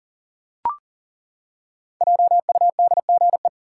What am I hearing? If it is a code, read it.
Morse code, heard as a keyed tone, about 40 wpm, 700 Hz. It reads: JUDGE
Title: Five Letter Words - 40wpm